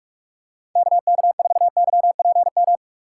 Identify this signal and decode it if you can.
Morse code, heard as a keyed tone, about 44 wpm, 690 Hz. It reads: KK4YPM
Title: USA Call Signs - 6x Character Spacing - 45wpm